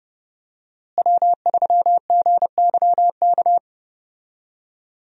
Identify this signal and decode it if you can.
Morse code, heard as a keyed tone, about 30 wpm, 700 Hz. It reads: W3GYK
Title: USA Call Signs - 5 Characters - Repeated 3x - 30wpm